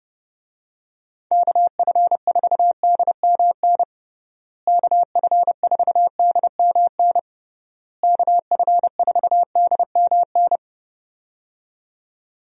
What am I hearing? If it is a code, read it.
Morse code, heard as a keyed tone, about 30 wpm, 700 Hz. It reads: KF4DMN KF4DMN KF4DMN
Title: USA Call Signs - 6 Characters - Repeated 3x - 30wpm